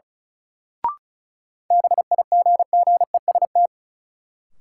Morse code, heard as a keyed tone, about 36 wpm, 700 Hz. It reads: BIGGEST
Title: Sets of 4 Words - New Word Drill - 35wpm